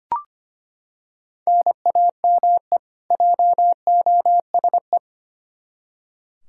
Morse code, heard as a keyed tone, about 25 wpm, 700 Hz. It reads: NAME JOSE
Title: Practice QSO - Exchange 1 - Beginning - 25wpm